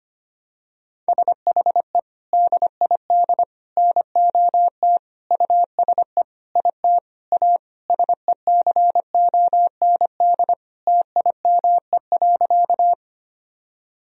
Morse code, heard as a keyed tone, about 25 wpm, 700 Hz. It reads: SHE DID NOT USE IT A SECOND TIME.
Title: Sentences from Top 1000 Words - Review Part 1 - 25wpm